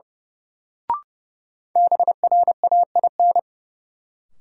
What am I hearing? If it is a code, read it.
Morse code, heard as a keyed tone, about 30 wpm, 700 Hz. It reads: BRAIN
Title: Sentences from Top 1100 Words - New Word Drill Alt - 30wpm